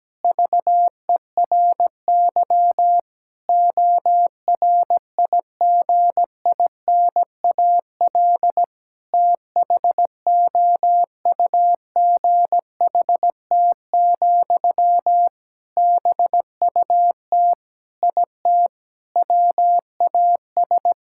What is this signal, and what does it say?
Morse code, heard as a keyed tone, about 17 wpm, 700 Hz. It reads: VERY ORIGINAL THOUGHT, BUT IT WAS
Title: Sentences from Top 1000 Words - Review Part 6 - 17wpm